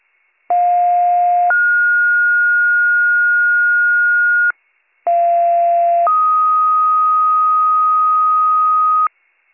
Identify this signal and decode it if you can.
Two-tone page, heard as a one-second tone, then a three-second tone, about 710 then 1430 Hz, about 690 then 1250 Hz, with noise. It is heard from 0.5 to 9.1 s.